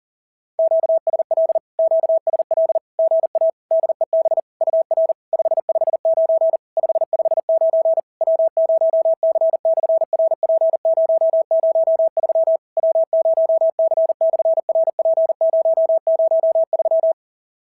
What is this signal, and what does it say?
Morse code, heard as a keyed tone, about 40 wpm, 650 Hz. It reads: QSL QSL GA DEB UR 559 559 W0C/RP003 W0C/RP003
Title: SOTA QSO Practice 40wpm